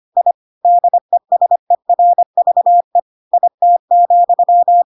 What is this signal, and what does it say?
Morse code, heard as a keyed tone, about 25 wpm, 700 Hz. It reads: I DESERVE IT,
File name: Pinocchio / Pinocchio ch3 25 25 ik1hge